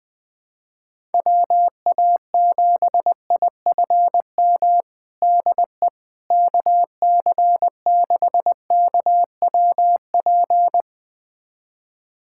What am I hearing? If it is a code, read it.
Morse code, heard as a keyed tone, about 20 wpm, 700 Hz. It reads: WA7IFM DE KC6KWP
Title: USA Callsign DE Callsign - 6 Characters - 20wpm